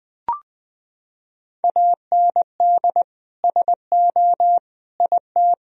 Morse code, heard as a keyed tone, about 20 wpm, 700 Hz. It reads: AND SO IT
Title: Sentences from Top 1000 Words - Review Part 1 - 20wpm